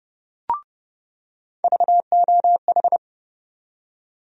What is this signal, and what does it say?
Morse code, heard as a keyed tone, about 29 wpm, 700 Hz. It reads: VOH
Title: USA Call Sign Postfixes - 3 Letters 30wpm